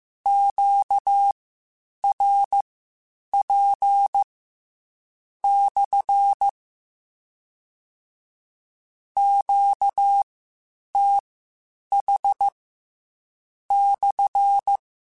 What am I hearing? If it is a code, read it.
Morse code, heard as a keyed tone, about 15 wpm, spaced as for 7 wpm, 800 Hz. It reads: QRP/ QTH/
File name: Q Calls / Q Calls-mp3s / Q CALLS SPSL1 @13